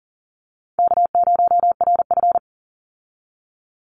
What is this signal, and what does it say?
Morse code, heard as a keyed tone, about 40 wpm, 700 Hz. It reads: K0RF
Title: Easy Call Signs Repeated 3x - 40wpm